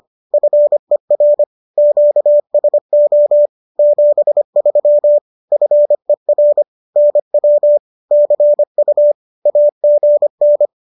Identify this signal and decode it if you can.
Morse code, heard as a keyed tone, about 25 wpm, 590 Hz. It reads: FER QSO 73 FER NW CU AGN
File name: qso201 25